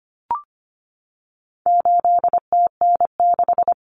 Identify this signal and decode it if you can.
Morse code, heard as a keyed tone, about 25 wpm, 700 Hz. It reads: 8TN6
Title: Common Cut Numbers - 4-Digit 25wpm